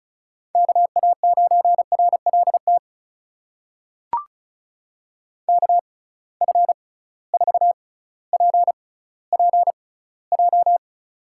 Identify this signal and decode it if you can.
Morse code, heard as a keyed tone, about 36 wpm, spaced as for 18 wpm, 700 Hz. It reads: KA9RLT KF4PPJ
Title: USA Call Signs - 6x Character Spacing - 35wpm